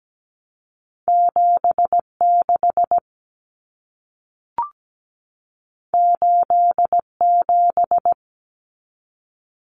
Morse code, heard as a keyed tone, about 17 wpm, 700 Hz. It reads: 76 87
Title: Common Cut Numbers - 2-Digit 17wpm